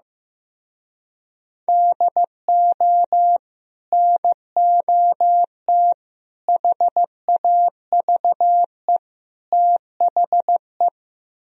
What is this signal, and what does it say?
Morse code, heard as a keyed tone, about 16 wpm, 700 Hz. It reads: DO NOT HAVE THE
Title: Speed-Racing - Sets of 4 Words 15wpm